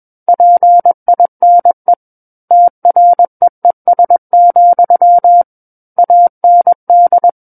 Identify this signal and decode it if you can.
Morse code, heard as a keyed tone, about 21 wpm, 700 Hz. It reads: PINE TREES, AND
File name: Book One / 12-Chapter 12 - 21 WPM